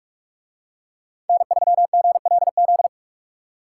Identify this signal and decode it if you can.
Morse code, heard as a keyed tone, about 44 wpm, 700 Hz. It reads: N3GLZ